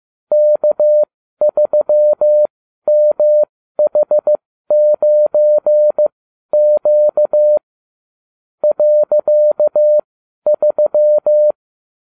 Morse code, heard as a keyed tone, about 15 wpm, spaced as for 12 wpm, 600 Hz. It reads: K3MH9Q .3